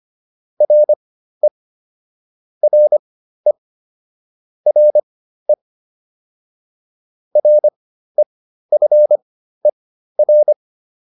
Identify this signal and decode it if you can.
Morse code, heard as a keyed tone, about 25 wpm, spaced as for 13 wpm, 600 Hz. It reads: RE RE RE REFER